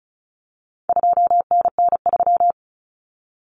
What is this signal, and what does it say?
Morse code, heard as a keyed tone, about 34 wpm, 700 Hz. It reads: 2NN3
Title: Common Cut Numbers - 4-Digit 35wpm